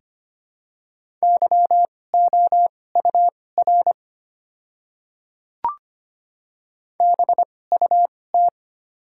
Morse code, heard as a keyed tone, about 25 wpm, spaced as for 18 wpm, 700 Hz. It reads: YOUR BUT
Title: 2x Character Spacing - Top 100 Words - 25wpm